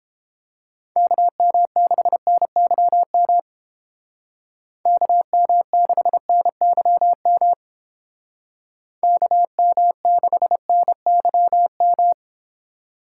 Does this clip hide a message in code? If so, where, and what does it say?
Yes: Morse code from 1.0 to 12.1 s, heard as a keyed tone, about 30 wpm, 700 Hz. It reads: KM6NYM KM6NYM KM6NYM